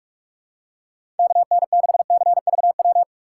Morse code, heard as a keyed tone, about 46 wpm, 700 Hz. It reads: KN6CVW
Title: USA Call Signs - 6 Characters - 3x Character Spacing - 45wpm